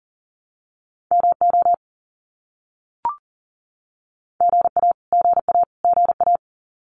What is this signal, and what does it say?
Morse code, heard as a keyed tone, about 40 wpm, 700 Hz. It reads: MO GA GA GA
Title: US States Repeated 3x 40wpm